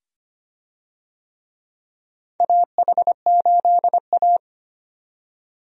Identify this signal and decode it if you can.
Morse code, heard as a keyed tone, about 25 wpm, 700 Hz. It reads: AH8A
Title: USA Call Signs - 4 Characters - 25wpm